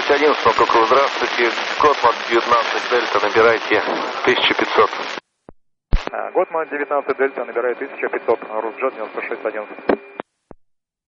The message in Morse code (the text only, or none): none